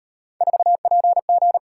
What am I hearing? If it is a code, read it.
Morse code, heard as a keyed tone, about 38 wpm, 700 Hz. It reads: 4PG